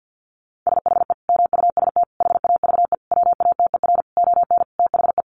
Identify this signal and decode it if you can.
Morse code, heard as a keyed tone, about 50 wpm, 700 Hz. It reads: SHE MUST HAVE WATER ON THE
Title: Sentences from Top 1100 Words 50wpm